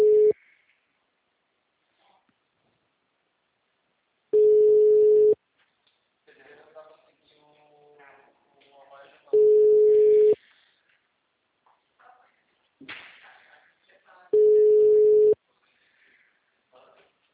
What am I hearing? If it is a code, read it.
Ringback tone (Europe, 425 Hz), with speech.
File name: out- -2010- - - .27897